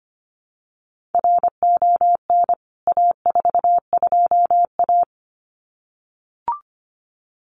Morse code, heard as a keyed tone, about 25 wpm, 700 Hz. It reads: RON A42A
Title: CWT Exchange 25wpm